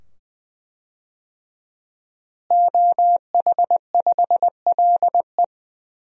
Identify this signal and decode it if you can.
Morse code, heard as a keyed tone, about 20 wpm, 700 Hz. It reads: OH5LE
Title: DXCC-100 Call Signs - 5 Characters - Repeated 2x - 20wpm